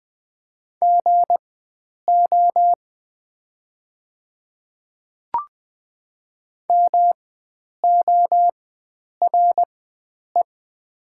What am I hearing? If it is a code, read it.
Morse code, heard as a keyed tone, about 20 wpm, spaced as for 9 wpm, 690 Hz. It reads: GO MORE
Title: 4x Character Spacing - Top 100 Words - 20wpm